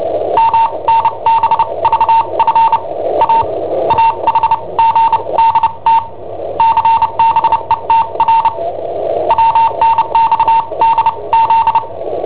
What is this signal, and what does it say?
Morse code, heard as a keyed tone, about 29 wpm, spaced as for 23 wpm, 910 Hz, with noise. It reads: MNBVF AAHGDT CBETR WNXDZ
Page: Cassetta esercizi preparata da I7OHP per IW5ELP